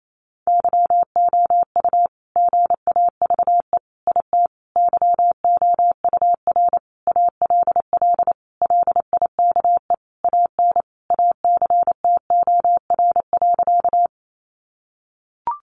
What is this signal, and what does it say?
Morse code, heard as a keyed tone, about 28 wpm, 700 Hz. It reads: YOU GAVE IT YOUR ALL LIKE AN ACTOR.
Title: Sentences from Top 1500 Words 28wpm